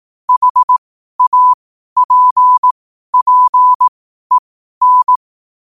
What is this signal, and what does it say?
Morse code, heard as a keyed tone, about 18 wpm, spaced as for 12 wpm, 1000 Hz. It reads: HAPPEN
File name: qotdfiles / 12wpm